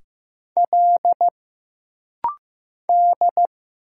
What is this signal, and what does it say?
Morse code, heard as a keyed tone, about 15 wpm, 700 Hz. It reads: L D
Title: Single Letters - Rapid-Fire 15wpm